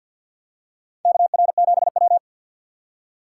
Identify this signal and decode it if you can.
Morse code, heard as a keyed tone, about 53 wpm, 700 Hz. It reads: KR7W
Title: K1USN SST QSO Practice - 50wpm